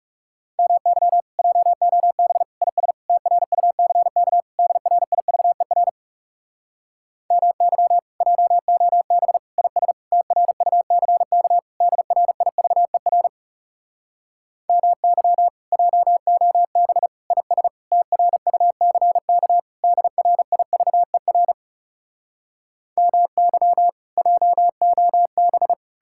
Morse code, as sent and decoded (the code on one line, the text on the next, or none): -- -.-- / .--- --- -... / .. ... / - .-. ..- -.-. -.- / -.. .-. .. ...- . .-. / -- -.-- / .--- --- -... / .. ... / - .-. ..- -.-. -.- / -.. .-. .. ...- . .-. / -- -.-- / .--- --- -... / .. ... / - .-. ..- -.-. -.- / -.. .-. .. ...- . .-. / -- -.-- / .--- --- -...
MY JOB IS TRUCK DRIVER MY JOB IS TRUCK DRIVER MY JOB IS TRUCK DRIVER MY JOB